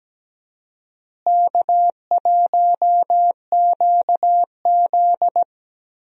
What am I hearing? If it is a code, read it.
Morse code, heard as a keyed tone, about 17 wpm, 700 Hz. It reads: K1QZ
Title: USA Call Signs - 4 Characters - Repeated 2x - 17wpm